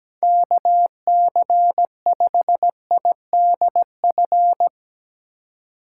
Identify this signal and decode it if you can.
Morse code, heard as a keyed tone, about 17 wpm, 700 Hz. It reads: KC5IDF
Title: USA Call Signs - 3x Character Spacing - 17wpm